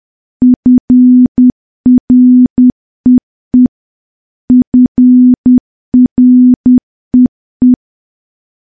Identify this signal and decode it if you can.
Morse code, heard as a keyed tone, about 10 wpm, 250 Hz. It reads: FREE FREE
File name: morse / morse-training-10WPM